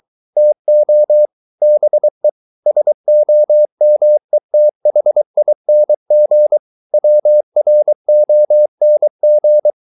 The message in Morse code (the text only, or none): TO BE SOMETHING WRONG